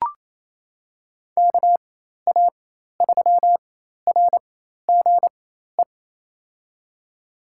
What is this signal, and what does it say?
Morse code, heard as a keyed tone, about 28 wpm, spaced as for 13 wpm, 700 Hz. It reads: KA3RGE
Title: USA Call Signs - 4x Character Spacing - 28wpm